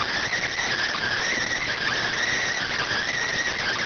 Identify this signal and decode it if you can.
Emergency Alert System (SAME) tones, heard as harsh buzzing bursts. It is heard from 0.3 to 3.1 s.